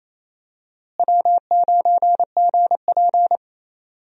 Morse code, heard as a keyed tone, about 28 wpm, 700 Hz. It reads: W9GP
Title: USA Call Signs - 4 Characters - Repeated 2x - 28wpm